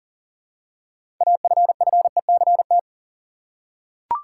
Morse code, heard as a keyed tone, about 40 wpm, 700 Hz. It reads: AFFECT
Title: Six Letter Words - 40wpm